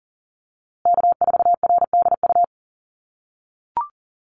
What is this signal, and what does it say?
Morse code, heard as a keyed tone, about 40 wpm, 700 Hz. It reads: K4RDU
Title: USA Call Signs Repeated 3x 40wpm